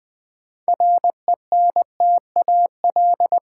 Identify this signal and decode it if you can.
Morse code, heard as a keyed tone, about 20 wpm, 700 Hz. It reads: RENTAL
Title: 6 Letter Words - First 3 Letters Given - 20wpm